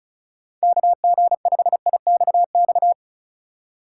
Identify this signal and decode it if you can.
Morse code, heard as a keyed tone, about 36 wpm, 700 Hz. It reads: KG5IXX
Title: USA Call Signs - 6 Characters - 2x Character Spacing - 35wpm